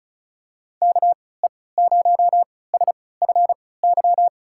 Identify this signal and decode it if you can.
Morse code, heard as a keyed tone, about 36 wpm, spaced as for 20 wpm, 700 Hz. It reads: KE0SFY
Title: USA Call Signs - 6 Characters - 3x Character Spacing - 35wpm